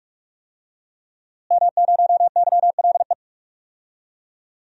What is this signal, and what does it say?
Morse code, heard as a keyed tone, about 48 wpm, 700 Hz. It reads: M0YLE
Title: DXCC-100 Call Signs - 5 Characters - Repeated 2x - 45wpm